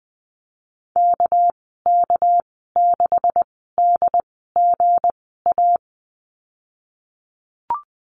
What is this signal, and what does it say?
Morse code, heard as a keyed tone, about 20 wpm, spaced as for 14 wpm, 700 Hz. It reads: KK6DGA
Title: 200 Lesson 32g - Letter k - Call Signs 20wpm